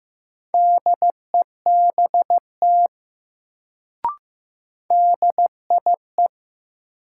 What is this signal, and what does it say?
Morse code, heard as a keyed tone, about 15 wpm, 700 Hz. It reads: DEBT DIE
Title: Sentences from Top 1300 Words - New Word Drill - 15wpm